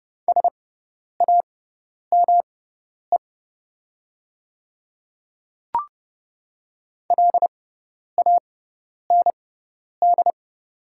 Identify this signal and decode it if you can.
Morse code, heard as a keyed tone, about 29 wpm, spaced as for 10 wpm, 700 Hz. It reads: SAME LAND
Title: 6x Character Spacing - Top 500 Words - 30wpm